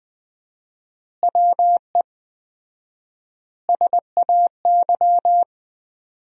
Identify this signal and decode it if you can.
Morse code, heard as a keyed tone, about 20 wpm, 700 Hz. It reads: WE SAY
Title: 4x Word-Spacing - Sentences from Top 200 Words - 20wpm